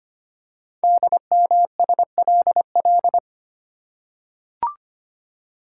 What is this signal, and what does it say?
Morse code, heard as a keyed tone, about 25 wpm, 700 Hz. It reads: DMSLL
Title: Code Groups - 5 Letter:Numbers - 25wpm